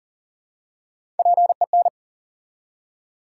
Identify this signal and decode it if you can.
Morse code, heard as a keyed tone, about 40 wpm, 700 Hz. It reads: PEN
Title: Three Letter Words - 40wpm